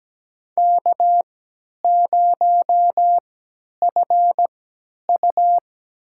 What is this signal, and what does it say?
Morse code, heard as a keyed tone, about 17 wpm, spaced as for 10 wpm, 700 Hz. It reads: K0FU
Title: USA Call Signs - 4 Characters - 3x Character Spacing - 17wpm